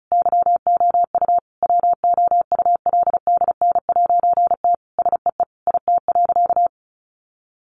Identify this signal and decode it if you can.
Morse code, heard as a keyed tone, about 34 wpm, 700 Hz. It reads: YOU WOULDN'T SEE IT.
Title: A Sound Of Thunder 35wpm